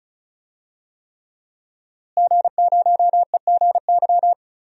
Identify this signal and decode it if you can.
Morse code, heard as a keyed tone, about 34 wpm, 700 Hz. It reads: G0EGY
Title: DXCC-100 Call Signs - 5 Characters - Repeated 3x - 35wpm